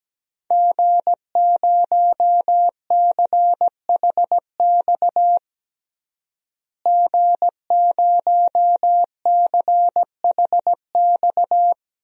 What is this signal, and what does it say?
Morse code, heard as a keyed tone, about 17 wpm, 700 Hz. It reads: G0CHX G0CHX
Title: DXCC-100 Call Signs - 5 Characters - Repeated 2x - 17wpm